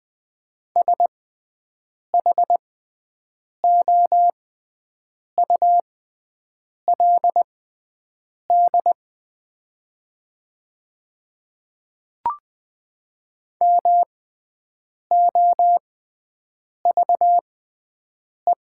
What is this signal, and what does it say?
Morse code, heard as a keyed tone, about 20 wpm, spaced as for 7 wpm, 690 Hz. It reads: SHOULD MOVE
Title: 6x Character Spacing - Top 300 Words - 20wpm